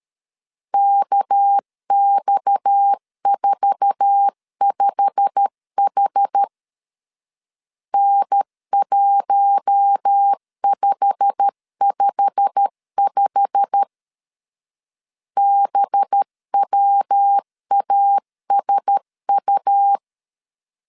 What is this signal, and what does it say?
Morse code, heard as a keyed tone, about 13 wpm, 800 Hz. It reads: KX45H N1555 BWASU
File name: LEZIONI / 8 YJ45 / 8 YJ45 12wpm 2151